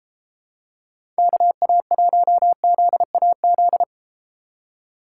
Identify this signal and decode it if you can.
Morse code, heard as a keyed tone, about 30 wpm, 700 Hz. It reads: KA1ZAZ